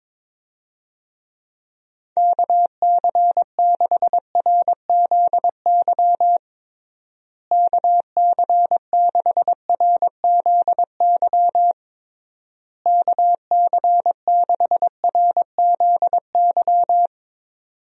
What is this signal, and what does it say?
Morse code, heard as a keyed tone, about 22 wpm, 700 Hz. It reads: KC6RZY KC6RZY KC6RZY
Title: USA Call Signs - 6 Characters - Repeated 3x - 22wpm